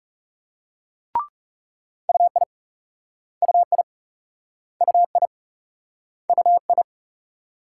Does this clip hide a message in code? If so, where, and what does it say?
Yes: Morse code from 2.1 to 6.8 s, heard as a keyed tone, about 38 wpm, 700 Hz. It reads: UI UI UI UI